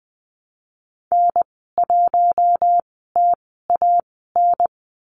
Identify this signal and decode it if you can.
Morse code, heard as a keyed tone, about 20 wpm, spaced as for 14 wpm, 700 Hz. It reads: N1TAN
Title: 048 Lesson 9g - Number 4 - Call Signs 20wpm